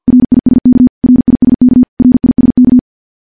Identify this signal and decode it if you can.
Morse code, heard as a keyed tone, about 50 wpm, 250 Hz. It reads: RISK RISK RISK
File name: morse / morse-training-50WPM